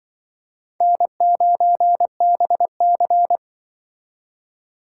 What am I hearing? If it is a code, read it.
Morse code, heard as a keyed tone, about 24 wpm, 700 Hz. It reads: N9BC
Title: Speed Racing - USA Call Signs 20wpm